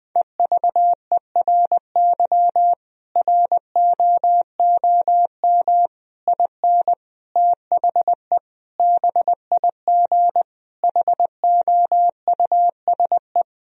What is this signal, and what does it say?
Morse code, heard as a keyed tone, about 20 wpm, 700 Hz. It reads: EVERY ROOM IN THE BIG HOUSE
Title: Sentences from Top 1000 Words - Review Part 3 - 20wpm